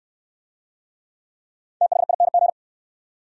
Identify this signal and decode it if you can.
Morse code, heard as a keyed tone, about 64 wpm, 700 Hz. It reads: THING